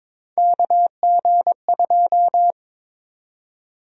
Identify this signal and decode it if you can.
Morse code, heard as a keyed tone, about 22 wpm, 700 Hz. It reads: KG2
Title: USA Call Sign Prefixes - 22wpm